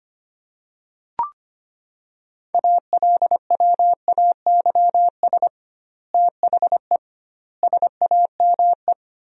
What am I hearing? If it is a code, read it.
Morse code, heard as a keyed tone, about 25 wpm, 700 Hz. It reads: ALWAYS THE SAME
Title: 2x Word-Spacing - Sentences from Top 300 Words - 25wpm